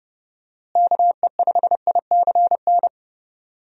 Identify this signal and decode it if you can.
Morse code, heard as a keyed tone, about 30 wpm, 700 Hz. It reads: KE5ICN
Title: USA Call Signs - 6 Characters - Repeated 2x - 30wpm